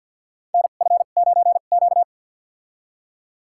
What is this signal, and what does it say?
Morse code, heard as a keyed tone, about 52 wpm, spaced as for 36 wpm, 700 Hz. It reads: NF9Q
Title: USA Call Signs - 4 Characters - 2x Character Spacing - 50wpm